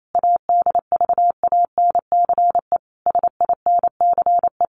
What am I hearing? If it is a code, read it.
Morse code, heard as a keyed tone, about 28 wpm, 700 Hz. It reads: ADVANCE SINCE
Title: Sentences from Top 1400 Words 28wpm